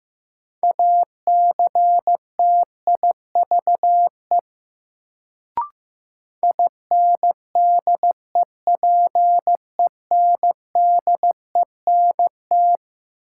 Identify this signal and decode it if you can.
Morse code, heard as a keyed tone, about 15 wpm, 700 Hz. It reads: ACTIVE INDEPENDENT
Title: Sentences from Top 900 Words - New Word Drill - 15wpm